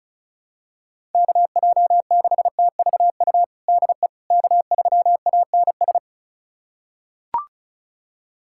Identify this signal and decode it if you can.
Morse code, heard as a keyed tone, about 36 wpm, 700 Hz. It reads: KJ6TVU DE K3ANS
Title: USA Callsign DE Callsign 35wpm